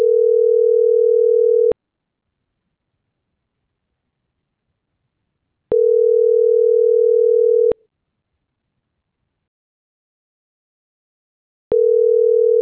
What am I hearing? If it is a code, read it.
Ringback tone (North America), 440 + 480 Hz.